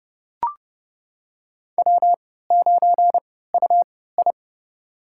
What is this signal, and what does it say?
Morse code, heard as a keyed tone, about 31 wpm, spaced as for 17 wpm, 700 Hz. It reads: W9UI
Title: USA Call Signs - 4 Characters - 3x Character Spacing - 30wpm